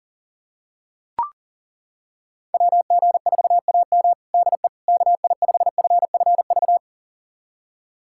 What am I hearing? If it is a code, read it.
Morse code, heard as a keyed tone, about 40 wpm, 700 Hz. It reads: WG4AM DE KI5FFV
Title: USA Callsign DE Callsign 40wpm